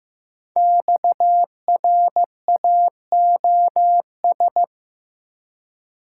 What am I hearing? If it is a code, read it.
Morse code, heard as a keyed tone, about 15 wpm, 700 Hz. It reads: XRAOS
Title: Code Groups - 5 Letters - 15wpm